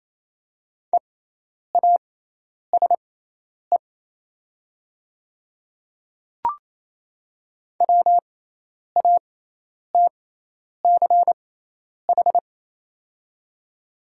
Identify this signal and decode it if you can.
Morse code, heard as a keyed tone, about 28 wpm, spaced as for 10 wpm, 700 Hz. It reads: EASE WATCH
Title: 6x Character Spacing - Top 300 Words - 28wpm